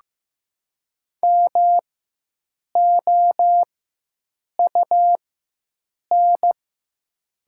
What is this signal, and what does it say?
Morse code, heard as a keyed tone, about 15 wpm, spaced as for 7 wpm, 700 Hz. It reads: MOUN
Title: 4x Character Spacing - Top 300 Words - 15wpm